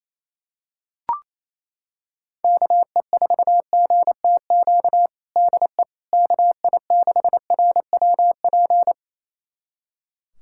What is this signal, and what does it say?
Morse code, heard as a keyed tone, about 28 wpm, 700 Hz. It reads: KE4GTQ DE KI6RWP
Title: USA Callsign DE Callsign - 6 Characters - 28wpm